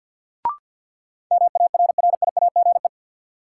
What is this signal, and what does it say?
Morse code, heard as a keyed tone, about 53 wpm, 700 Hz. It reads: MARRIAGE
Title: Sentences from Top 800 Words - New Word Drill - 50wpm